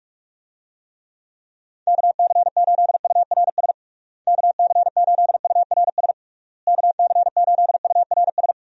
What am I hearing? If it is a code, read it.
Morse code, heard as a keyed tone, about 46 wpm, 700 Hz. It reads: KC8URS KC8URS KC8URS
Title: USA Call Signs - 6 Characters - Repeated 3x - 45wpm